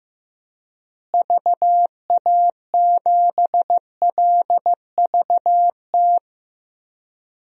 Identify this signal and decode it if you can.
Morse code, heard as a keyed tone, about 15 wpm, 700 Hz. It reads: VA7LVT
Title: DXCC-100 Call Signs - 6 Characters - Repeated 3x - 15wpm